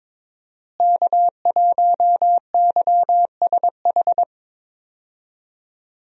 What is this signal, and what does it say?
Morse code, heard as a keyed tone, about 22 wpm, 700 Hz. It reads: K1YSH
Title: USA Call Signs - 5 Characters - Repeated 2x - 22wpm